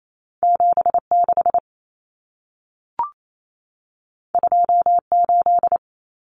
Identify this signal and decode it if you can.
Morse code, heard as a keyed tone, about 28 wpm, 700 Hz. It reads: 76 28
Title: Numbers - 2-Digit 28wpm